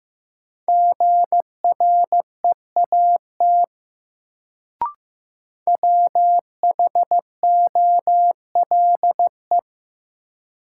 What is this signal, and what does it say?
Morse code, heard as a keyed tone, about 15 wpm, 700 Hz. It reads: GREAT WHOLE
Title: Sets of 2 Words - New Word Drill - 15wpm